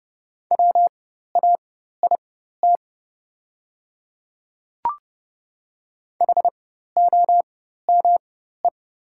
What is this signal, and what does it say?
Morse code, heard as a keyed tone, about 30 wpm, spaced as for 14 wpm, 690 Hz. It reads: WAIT HOME
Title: 4x Character Spacing - Top 500 Words - 30wpm